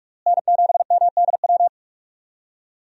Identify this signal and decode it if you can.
Morse code, heard as a keyed tone, about 45 wpm, 700 Hz. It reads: N7MDW
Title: USA Call Signs - 5 Characters - Repeated 2x - 45wpm